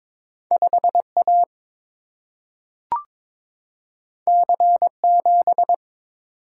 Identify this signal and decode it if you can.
Morse code, heard as a keyed tone, about 22 wpm, 700 Hz. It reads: 5 A C 7
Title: Single-Letter Single-Number 22wpm